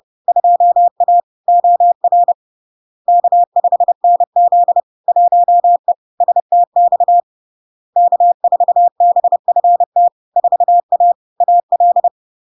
Morse code, heard as a keyed tone, about 30 wpm, 700 Hz. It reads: 2A OR K5NZ 1E STX K4BFT 4A AL
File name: LARK-FD-2022-Log-in-CW-30wpm-700-Hz